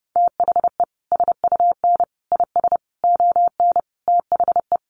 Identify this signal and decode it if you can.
Morse code, heard as a keyed tone, about 30 wpm, 700 Hz. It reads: THE SUN IS ON THE
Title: Sentences from Top 1300 Words 30wpm